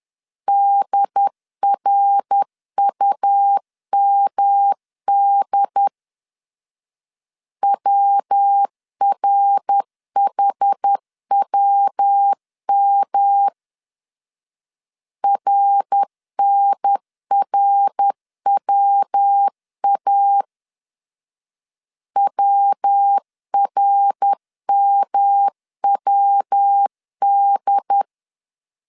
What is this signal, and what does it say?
Morse code, heard as a keyed tone, about 10 wpm, 800 Hz. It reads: DRUMD WRHWM RNRWA WRMWD
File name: CW / DET / 1 3 1500C S C 10wpm 3335